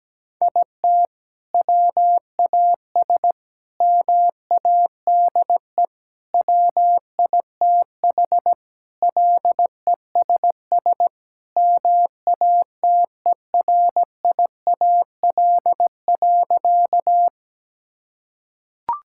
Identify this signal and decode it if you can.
Morse code, heard as a keyed tone, about 17 wpm, 700 Hz. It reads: IT WAS MADE WITH LESS MATERIAL.
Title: Sentences from Top 1000 Words - Review Part 5 - 17wpm